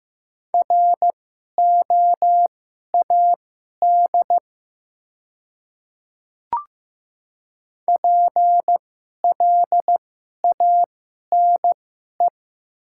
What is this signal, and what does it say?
Morse code, heard as a keyed tone, about 15 wpm, spaced as for 11 wpm, 690 Hz. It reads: ROAD PLANE
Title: 2x Character Spacing - Top 500 Words - 15wpm